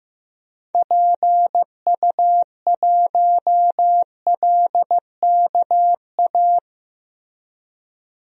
Morse code, heard as a keyed tone, about 15 wpm, 700 Hz. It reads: PU1LKA
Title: DXCC-100 Call Signs - 6 Characters - Repeated 3x - 15wpm